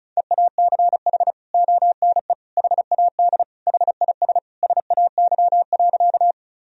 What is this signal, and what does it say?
Morse code, heard as a keyed tone, about 36 wpm, 700 Hz. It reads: EACH ONE HAD HIS SAY.
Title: Sentences from Top 1000 Words - Review Part 1 - 35wpm